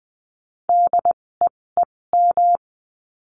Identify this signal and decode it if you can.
Morse code, heard as a keyed tone, about 20 wpm, spaced as for 16 wpm, 700 Hz. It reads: DEEM
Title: 256 Lesson 42a - Words 12-20wpm